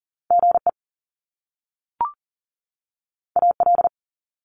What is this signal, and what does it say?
Morse code, heard as a keyed tone, about 40 wpm, 700 Hz. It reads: GE AL
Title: Top Two Letter Combinations in Words 40wpm